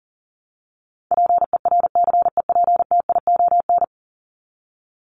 Morse code, heard as a keyed tone, about 40 wpm, 700 Hz. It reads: PERCEPTION
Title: Top 1751-2000 Words 40wpm